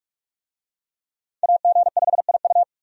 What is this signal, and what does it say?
Morse code, heard as a keyed tone, about 46 wpm, 700 Hz. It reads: AG5IU
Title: USA Call Signs - 5 Characters - 45wpm